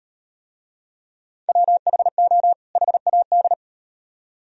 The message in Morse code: WHO HAD

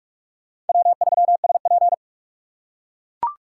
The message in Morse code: W3SP

